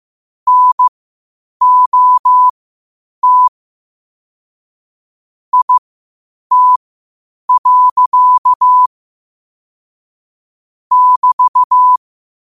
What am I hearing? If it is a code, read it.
Morse code, heard as a keyed tone, about 15 wpm, spaced as for 8 wpm, 1000 Hz. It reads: NOT IT. <BT>
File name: qotdfiles / 8wpm